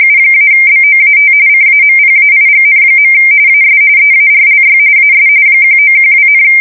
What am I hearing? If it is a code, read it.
RTTY, about 45.45 baud. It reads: U-BOAT CAMPAIGN. AS EARLY AS 1912,
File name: fsk170 / usnavy4